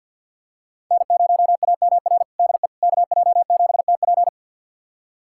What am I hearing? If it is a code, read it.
Morse code, heard as a keyed tone, about 51 wpm, 700 Hz. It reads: N0AMR DE KJ7TP
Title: USA Callsign DE Callsign - 5 Characters - 50wpm